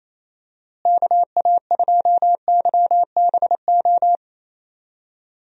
Morse code, heard as a keyed tone, about 27 wpm, 700 Hz. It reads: KA2YBO
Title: USA Call Signs - 6 Characters - Repeated 2x - 28wpm